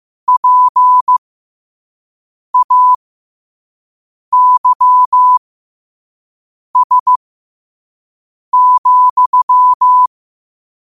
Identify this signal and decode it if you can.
Morse code, heard as a keyed tone, about 15 wpm, spaced as for 5 wpm, 1000 Hz. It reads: PAYS,
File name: qotdfiles / 5wpm